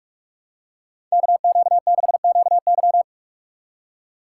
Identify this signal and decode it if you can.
Morse code, heard as a keyed tone, about 44 wpm, 700 Hz. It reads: KQ6QY